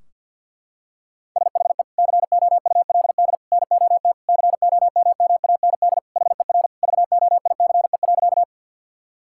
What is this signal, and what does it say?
Morse code, heard as a keyed tone, about 50 wpm, 700 Hz. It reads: SHE COULD NOT COMMAND HER VOICE.
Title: Sentences from Top 1600 Words 50wpm